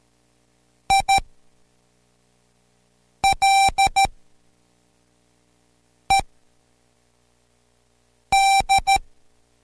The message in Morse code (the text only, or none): ILED